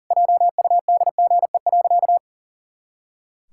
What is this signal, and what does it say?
Morse code, heard as a keyed tone, about 37 wpm, 700 Hz. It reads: JUDGE.
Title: Pangram Challenge v2 - 40wpm